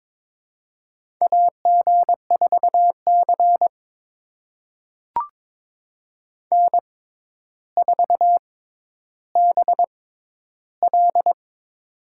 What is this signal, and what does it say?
Morse code, heard as a keyed tone, about 22 wpm, spaced as for 11 wpm, 700 Hz. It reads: AG4C N4BL
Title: USA Call Signs - 4 Characters - 6x Character Spacing - 22wpm